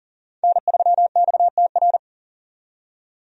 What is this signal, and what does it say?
Morse code, heard as a keyed tone, about 40 wpm, 700 Hz. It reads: N3XTR
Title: USA Call Signs - 5 Characters - 40wpm